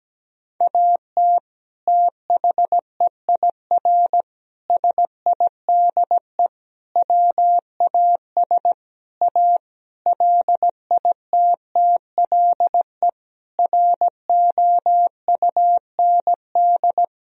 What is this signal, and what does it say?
Morse code, heard as a keyed tone, about 17 wpm, 700 Hz. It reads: AT THEIR SIDE WAS A LITTLE ROUND
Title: Sentences from Top 1000 Words - Review Part 2 - 17wpm